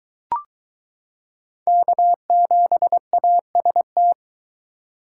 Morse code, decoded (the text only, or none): K7AST